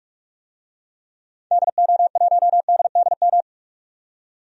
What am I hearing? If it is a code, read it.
Morse code, heard as a keyed tone, about 41 wpm, 700 Hz. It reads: DO1DDM
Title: DXCC-100 Call Signs - 6 Characters - 45wpm